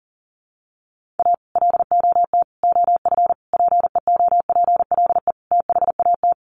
Morse code, heard as a keyed tone, about 38 wpm, 700 Hz. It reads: A LOT OF PEOPLE THAT
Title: Sets of 5 Words Encore2 40wpm